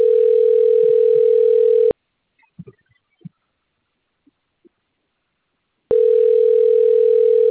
Ringback tone (North America), 440 + 480 Hz.
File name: rg-1600-unknown- - - .76311